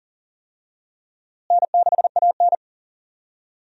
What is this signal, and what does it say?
Morse code, heard as a keyed tone, about 37 wpm, 700 Hz. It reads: N6AN
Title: USA Call Signs - 4 Characters - Repeated 2x - 40wpm